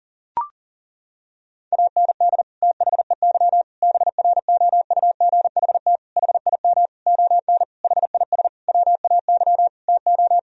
Morse code, heard as a keyed tone, about 40 wpm, 700 Hz. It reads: AND THEY BROUGHT HIM ON HIS WAY TO
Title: Sentences from Top 1000 Words - Review Part 2 - 40wpm